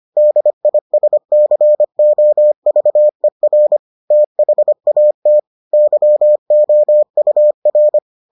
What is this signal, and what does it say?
Morse code, heard as a keyed tone, about 25 wpm, 600 Hz. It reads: DISCOVER THAT YOUR